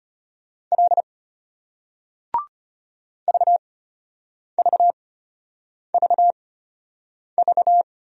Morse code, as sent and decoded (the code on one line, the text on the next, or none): .-.. / ...- ...- ...- ...-
L VVVV